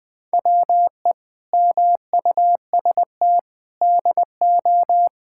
Morse code, heard as a keyed tone, about 21 wpm, 700 Hz. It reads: WE MUST DO
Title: Sentences from Top 1000 Words - Review Part 5 - 20wpm